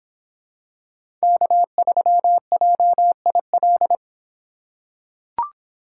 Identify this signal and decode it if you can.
Morse code, heard as a keyed tone, about 26 wpm, 700 Hz. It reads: K3JIL